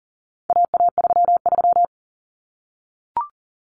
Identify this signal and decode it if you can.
Morse code, heard as a keyed tone, about 40 wpm, 700 Hz. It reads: AA33
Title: Common Cut Numbers - 4-Digit 40wpm